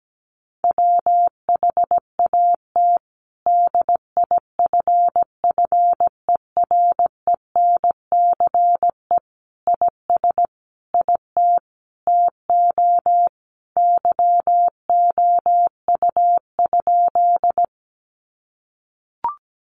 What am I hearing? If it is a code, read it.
Morse code, heard as a keyed tone, about 17 wpm, 700 Hz. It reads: WHAT DIFFERENCE IS IT TO YOU?
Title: Sentences from Top 800 Words 17wpm